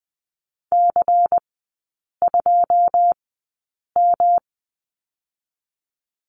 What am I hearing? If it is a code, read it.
Morse code, heard as a keyed tone, about 20 wpm, spaced as for 8 wpm, 700 Hz. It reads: C2M
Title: 172 Lesson 28e - Number 9 - Three Characters 20wpm